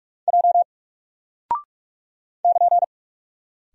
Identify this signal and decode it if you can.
Morse code, heard as a keyed tone, about 44 wpm, 700 Hz. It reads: J <KN>
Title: Single Character (Letter:Number:Punctuation:Prosign) Alt 45wpm